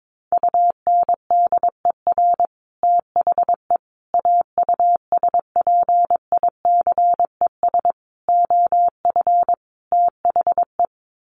Morse code, heard as a keyed tone, about 22 wpm, 700 Hz. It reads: UNDER THE AUSPICES OF THE
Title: Sets of 5 Words Encore2 22wpm